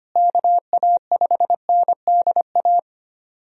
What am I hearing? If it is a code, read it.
Morse code, heard as a keyed tone, about 25 wpm, 700 Hz. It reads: KA5NDA
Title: USA Call Signs - 6 Characters - Repeated 3x - 25wpm